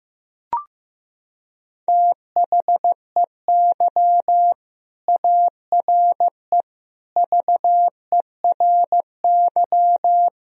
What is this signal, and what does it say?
Morse code, heard as a keyed tone, about 15 wpm, 700 Hz. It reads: THEY ARE VERY
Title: Sentences from Top 1000 Words - Review Part 3 - 15wpm